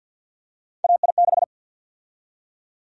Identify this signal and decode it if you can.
Morse code, heard as a keyed tone, about 47 wpm, 700 Hz. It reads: AI6
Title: USA Call Sign Prefixes - 50wpm